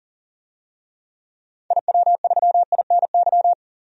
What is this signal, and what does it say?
Morse code, heard as a keyed tone, about 40 wpm, 700 Hz. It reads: IW3INY
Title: DXCC-100 Call Signs - 6 Characters - 40wpm